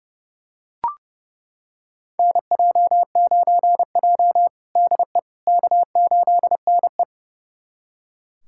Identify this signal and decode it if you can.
Morse code, heard as a keyed tone, about 30 wpm, 700 Hz. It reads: NJ9J DE K8NE
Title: USA Callsign DE Callsign - 4 Characters - 30wpm